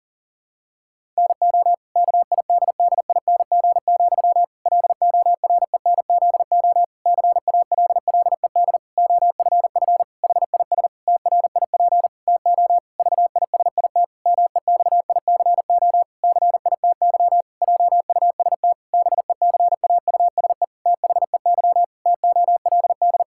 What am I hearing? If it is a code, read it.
Morse code, heard as a keyed tone, about 40 wpm, 700 Hz. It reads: NO KIDDING, LORENZO CALLED OFF HIS TRIP TO VISIT MEXICO CITY JUST BECAUSE THEY TOLD
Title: Pangram Challenge v2 - 40wpm